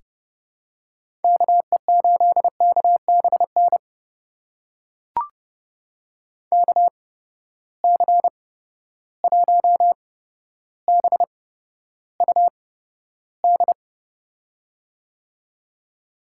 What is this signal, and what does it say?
Morse code, heard as a keyed tone, about 30 wpm, spaced as for 13 wpm, 700 Hz. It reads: KE8KBN KC1BUD
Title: USA Call Signs - 6 Characters - 8x Character Spacing - 30wpm